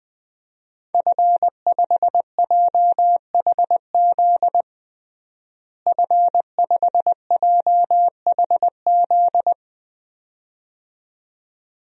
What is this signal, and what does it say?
Morse code, heard as a keyed tone, about 21 wpm, 700 Hz. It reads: F5JHZ F5JHZ
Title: DXCC-100 Call Signs - 5 Characters - Repeated 2x - 20wpm